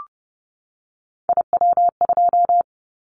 Morse code, heard as a keyed tone, about 30 wpm, 700 Hz. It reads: IW2
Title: Callsign Prefixes for Top 25 DXCC - 30wpm